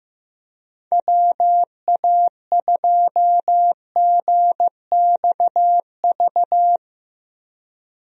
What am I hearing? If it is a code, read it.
Morse code, heard as a keyed tone, about 16 wpm, 700 Hz. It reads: WA2GXV
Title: USA Call Signs - 6 Characters - Repeated 2x - 15wpm